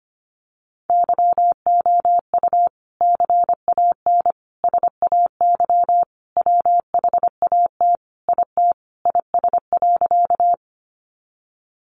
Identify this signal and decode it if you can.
Morse code, heard as a keyed tone, about 25 wpm, 700 Hz. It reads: YOU CAN SAY WHAT IT IS.
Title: Sentences from Top 200 Words 25wpm